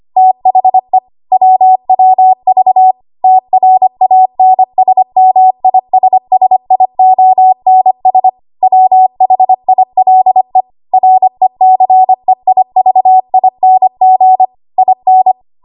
Morse code, heard as a keyed tone, about 25 wpm, 750 Hz. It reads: THE WWV TRANSMISSIONS WHILE RECEIVING IN
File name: Archive / 25 WPM / 25WPM